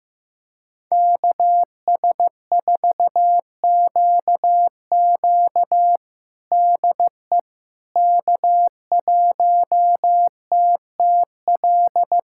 Morse code, heard as a keyed tone, about 15 wpm, 700 Hz. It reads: KS4QQ DE K1TTL
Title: USA Callsign DE Callsign - 5 Characters - 15wpm